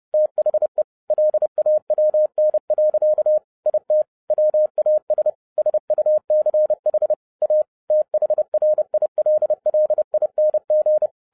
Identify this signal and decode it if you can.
Morse code, heard as a keyed tone, about 30 wpm, 610 Hz. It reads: THE LAWN. IT WAS SUCH A THRILLING